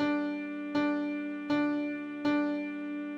There is no code.